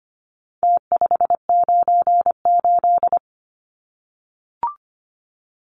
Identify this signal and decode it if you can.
Morse code, heard as a keyed tone, about 25 wpm, 700 Hz. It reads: T598
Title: Numbers - Contest Style - 4-Digit 25wpm